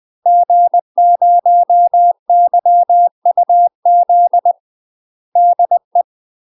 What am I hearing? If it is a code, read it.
Morse code, heard as a keyed tone, about 20 wpm, 700 Hz. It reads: G0YUZ DE
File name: UK MASTER QSOs TRAINING / 20WPM / UK MASTER QSOs 0007